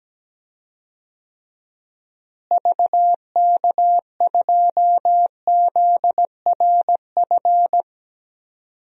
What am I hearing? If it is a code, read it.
Morse code, heard as a keyed tone, about 17 wpm, 700 Hz. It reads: VK2ZRF
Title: DXCC-100 Call Signs - 6 Characters - Repeated 2x - 17wpm